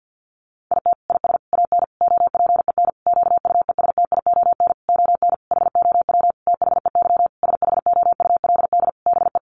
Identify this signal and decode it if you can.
Morse code, heard as a keyed tone, about 50 wpm, 700 Hz. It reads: IT IS AN OPEN QUESTION ON HOW THEY SHOULD BE
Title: Sentences from Top 1300 Words 50wpm